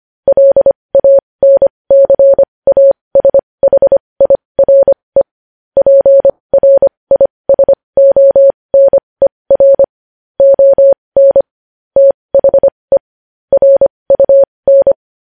Morse code, heard as a keyed tone, about 25 wpm, spaced as for 20 wpm, 550 Hz. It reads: LANCASHIRE PRISONER ON THE RUN
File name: podcast / News headlines-1x-25x20